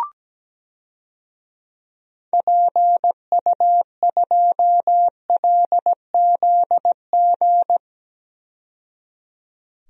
Morse code, heard as a keyed tone, about 17 wpm, 700 Hz. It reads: PU2LZG
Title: DXCC-100 Call Signs - 6 Characters - 17wpm